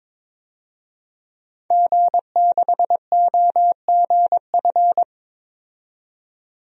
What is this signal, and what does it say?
Morse code, heard as a keyed tone, about 22 wpm, 700 Hz. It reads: G6OGF
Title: DXCC-100 Call Signs - 5 Characters - 22wpm